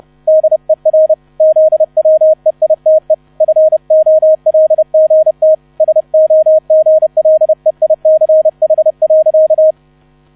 Morse code, heard as a keyed tone, about 32 wpm, 630 Hz, with radio noise. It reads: DER ZWEITE FOLGT SOGLEICH.